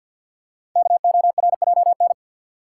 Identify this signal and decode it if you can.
Morse code, heard as a keyed tone, about 48 wpm, 700 Hz. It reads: KORJN
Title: Code Groups - 5 Letters - 50wpm